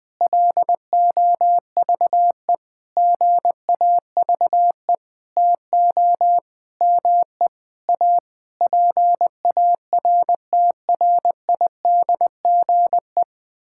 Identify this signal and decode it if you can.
Morse code, heard as a keyed tone, about 20 wpm, 700 Hz. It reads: LOVE GAVE TO ME A PARTRIDGE
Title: Sentences with 9-Word Phrases - Spoken Beginning - 20wpm